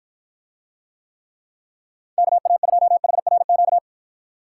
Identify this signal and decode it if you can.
Morse code, heard as a keyed tone, about 54 wpm, 700 Hz. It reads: KA2HRQ